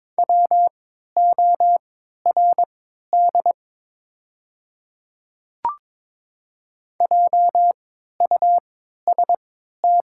Morse code, heard as a keyed tone, about 22 wpm, spaced as for 12 wpm, 700 Hz. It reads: WORD JUST
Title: 3x Character Spacing - Top 200 Words - 22wpm